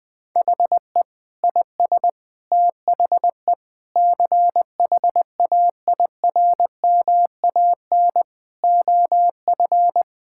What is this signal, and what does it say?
Morse code, heard as a keyed tone, about 20 wpm, 700 Hz. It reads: HE IS THE CHAIRMAN OF
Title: Sentences with 3-Word Phrases - 20wpm